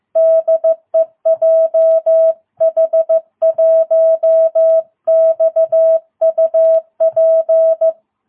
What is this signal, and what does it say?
Morse code, heard as a keyed tone, about 14 wpm, 660 Hz. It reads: DE JH1XUP
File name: jh8cbh / 2022cw